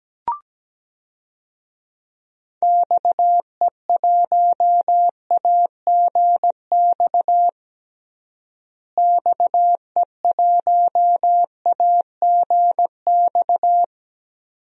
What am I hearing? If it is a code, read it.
Morse code, heard as a keyed tone, about 17 wpm, 700 Hz. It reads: XE1AGX XE1AGX
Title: DXCC-100 Call Signs - 6 Characters - Repeated 2x - 17wpm